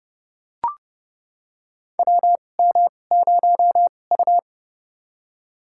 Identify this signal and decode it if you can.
Morse code, heard as a keyed tone, about 30 wpm, spaced as for 22 wpm, 700 Hz. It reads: WM0U
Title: USA Call Signs - 4 Characters - 2x Character Spacing - 30wpm